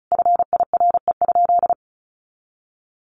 Morse code, heard as a keyed tone, about 34 wpm, 700 Hz. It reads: FIRE?
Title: Sentences from Top 1000 Words 35wpm